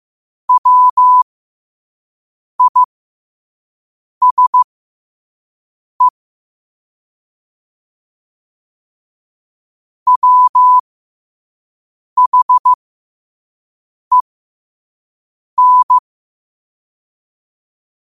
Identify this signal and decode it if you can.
Morse code, heard as a keyed tone, about 15 wpm, spaced as for 5 wpm, 1000 Hz. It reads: WISE WHEN
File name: qotdfiles / 5wpm